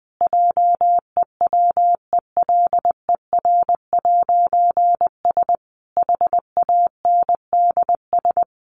Morse code, heard as a keyed tone, about 20 wpm, 700 Hz. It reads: JEWELER'S HANDS
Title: A Sound Of Thunder 20wpm